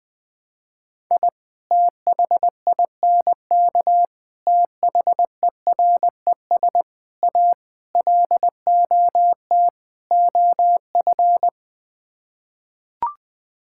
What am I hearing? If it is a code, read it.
Morse code, heard as a keyed tone, about 20 wpm, 700 Hz. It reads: I THINK THERES A LOT OF
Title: Sets of 6 Words - 20wpm